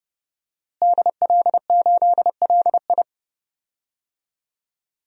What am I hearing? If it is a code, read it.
Morse code, heard as a keyed tone, about 30 wpm, 700 Hz. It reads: DL8LI
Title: DXCC-100 Call Signs - 5 Characters - Repeated 3x - 30wpm